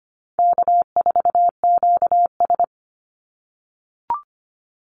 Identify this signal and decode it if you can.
Morse code, heard as a keyed tone, about 25 wpm, 700 Hz. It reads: K4QS
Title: NA SOTA Chaser Callsigns 25wpm